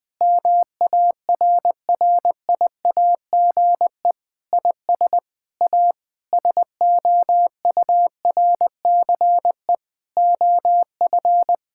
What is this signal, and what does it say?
Morse code, heard as a keyed tone, about 21 wpm, 700 Hz. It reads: MARRIAGE IS A SOURCE OF
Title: Sentences from Top 1000 Words - Review Part 6 - 20wpm